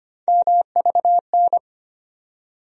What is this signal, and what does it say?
Morse code, heard as a keyed tone, about 25 wpm, 700 Hz. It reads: MVN
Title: USA Call Sign Postfixes - 3 Letters 25wpm